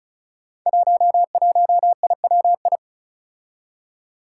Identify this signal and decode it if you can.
Morse code, heard as a keyed tone, about 36 wpm, 700 Hz. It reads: 11IWI
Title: Five Challenging Letter-Number Combinations 35wpm